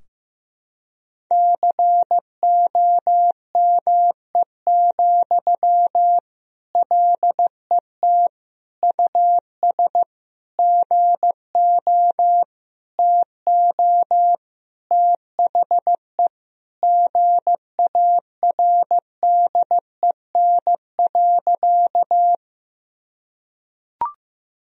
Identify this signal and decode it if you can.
Morse code, heard as a keyed tone, about 15 wpm, 700 Hz. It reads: COME, LET US GO TO THE GARDEN.
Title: Sentences from Top 1000 Words - Review Part 4 - 15wpm